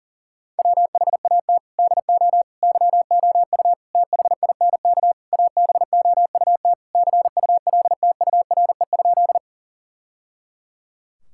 Morse code, heard as a keyed tone, about 37 wpm, 700 Hz. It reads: WHAT DO YOU THINK ABOUT CULTURE?
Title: Sentences from Top 1000 Words - Review Part 6 - 40wpm